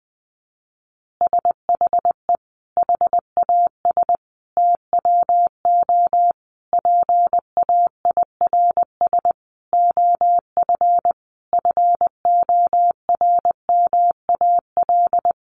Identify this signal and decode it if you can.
Morse code, heard as a keyed tone, about 20 wpm, 700 Hz. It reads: SHE HAS TWO PAIRS OF FORMAL
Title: Additional Easy Phrases 20wpm